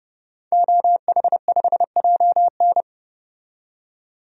Morse code, heard as a keyed tone, about 31 wpm, 700 Hz. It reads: OH5JN
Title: DXCC-100 Call Signs - 5 Characters - 30wpm